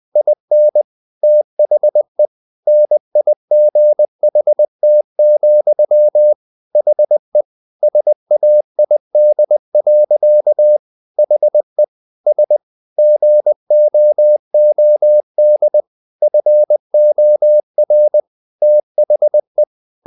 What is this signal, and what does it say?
Morse code, heard as a keyed tone, about 20 wpm, 600 Hz. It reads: IN THE NIGHT, HE SAID. HE S GOOD FOR THE
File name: the old man and the sea part 2 20wpm